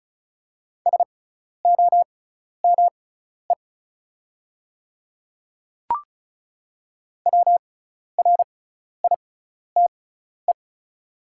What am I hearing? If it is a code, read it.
Morse code, heard as a keyed tone, about 36 wpm, spaced as for 12 wpm, 700 Hz. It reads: SOME WRITE
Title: 6x Character Spacing - Top 100 Words - 35wpm